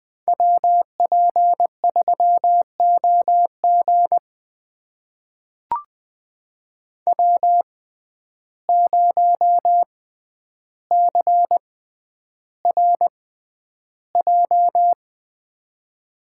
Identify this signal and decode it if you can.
Morse code, heard as a keyed tone, about 21 wpm, spaced as for 10 wpm, 700 Hz. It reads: WP3OG W0CRJ
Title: USA Call Signs - 5 Characters - 6x Character Spacing - 20wpm